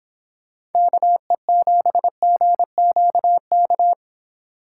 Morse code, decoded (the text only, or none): KE7GQK